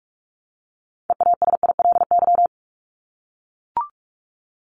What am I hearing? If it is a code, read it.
Morse code, heard as a keyed tone, about 44 wpm, 700 Hz. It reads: EASILY
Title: LY Suffix - Top 100 Words with Base Words 45wpm